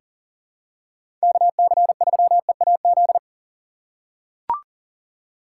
Morse code, heard as a keyed tone, about 40 wpm, 700 Hz. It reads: KC3EAZ
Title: USA Call Signs - 6 Characters - 3x Character Spacing - 40wpm